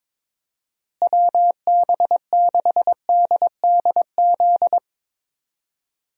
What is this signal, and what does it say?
Morse code, heard as a keyed tone, about 22 wpm, 700 Hz. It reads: WB6DDZ
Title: USA Call Signs - 6 Characters - Repeated 3x - 22wpm